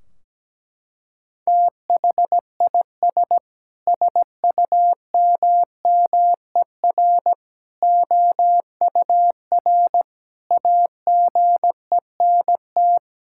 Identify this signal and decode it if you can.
Morse code, heard as a keyed tone, about 17 wpm, 700 Hz. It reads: THIS SUMMER OUR AGENT
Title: Sentences from Top 1600 Words 17wpm